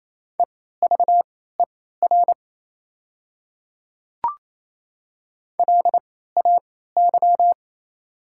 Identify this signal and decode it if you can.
Morse code, heard as a keyed tone, about 28 wpm, spaced as for 16 wpm, 690 Hz. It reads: EVER LAY
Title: 3x Character Spacing - Top 500 Words - 28wpm